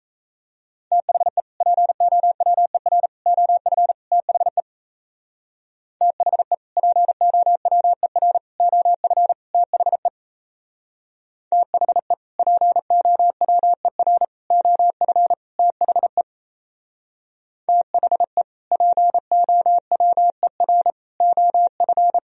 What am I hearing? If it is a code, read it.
Morse code, heard as a keyed tone, about 36 wpm, 700 Hz. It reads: THE POWER OF THE THE POWER OF THE THE POWER OF THE THE POWER OF